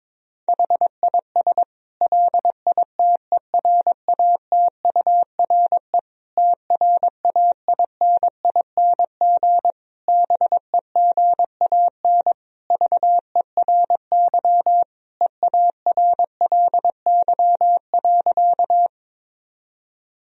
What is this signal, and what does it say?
Morse code, heard as a keyed tone, about 22 wpm, 700 Hz. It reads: HIS LITERATURE TRAINING BEGAN VERY EARLY.
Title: Sentences from Top 1000 Words - Review Part 4 - 22wpm